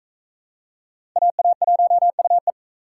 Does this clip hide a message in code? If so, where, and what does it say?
Yes: Morse code from 1.2 to 2.5 s, heard as a keyed tone, about 41 wpm, 700 Hz. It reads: AA1UE